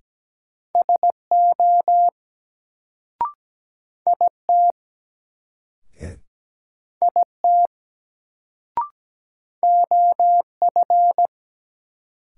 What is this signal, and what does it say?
Morse code, heard as a keyed tone, about 17 wpm, 700 Hz. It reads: SO IT IT OF
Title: Sentences from Top 100 Words - New Word Drill - 17wpm